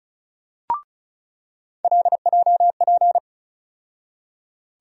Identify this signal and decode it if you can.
Morse code, heard as a keyed tone, about 34 wpm, 700 Hz. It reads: LJP
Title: USA Call Sign Postfixes - 3 Letters 35wpm